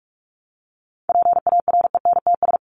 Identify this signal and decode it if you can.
Morse code, heard as a keyed tone, about 44 wpm, 700 Hz. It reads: PARENTS
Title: ES:S Suffix - Top 100 Words with Base Words 45wpm